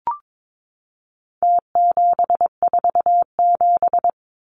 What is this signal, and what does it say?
Morse code, heard as a keyed tone, about 22 wpm, 700 Hz. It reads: T747
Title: Numbers - Contest Style - 4-Digit 22wpm